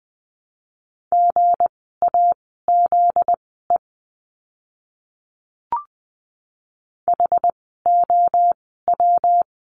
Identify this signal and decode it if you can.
Morse code, heard as a keyed tone, about 20 wpm, spaced as for 15 wpm, 700 Hz. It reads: GAZE HOW
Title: 244 Lesson 39f - Letter z - Words 20wpm